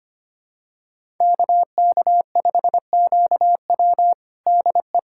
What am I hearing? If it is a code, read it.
Morse code, heard as a keyed tone, about 25 wpm, 700 Hz. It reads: KK5QW DE
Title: USA Callsign DE Callsign - 5 Characters - 25wpm